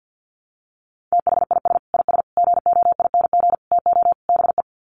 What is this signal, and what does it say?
Morse code, heard as a keyed tone, about 50 wpm, 700 Hz. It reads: THIS IS GOING TO BE
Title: Sets of 5 Words 50wpm